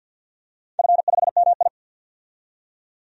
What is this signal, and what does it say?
Morse code, heard as a keyed tone, about 50 wpm, 700 Hz. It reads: F5MI
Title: DXCC-100 Call Signs - 4 Characters - 50wpm